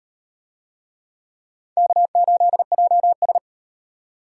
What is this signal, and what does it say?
Morse code, heard as a keyed tone, about 35 wpm, 700 Hz. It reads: K8JS